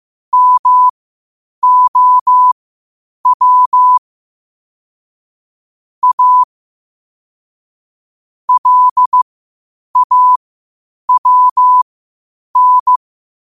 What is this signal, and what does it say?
Morse code, heard as a keyed tone, about 15 wpm, spaced as for 8 wpm, 1000 Hz. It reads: MOW A LAWN
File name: qotdfiles / 8wpm